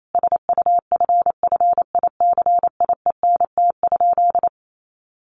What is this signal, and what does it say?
Morse code, heard as a keyed tone, about 28 wpm, 700 Hz. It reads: SUFFICIENT?
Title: Sentences from Top 1400 Words 28wpm